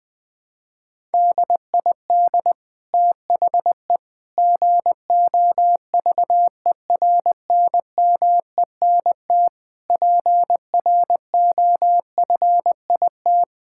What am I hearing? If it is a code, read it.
Morse code, heard as a keyed tone, about 20 wpm, 700 Hz. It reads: DID THE GOVERNMENT PROFIT
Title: Sentences from Top 1000 Words - Review Part 5 - 20wpm